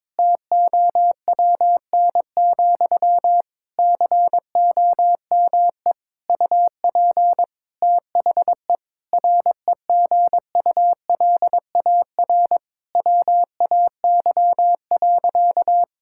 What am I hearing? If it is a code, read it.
Morse code, heard as a keyed tone, about 22 wpm, 700 Hz. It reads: TOWN, COME UP THE REGULAR WAY.
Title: Sentences from Top 1000 Words - Review Part 6 - 22wpm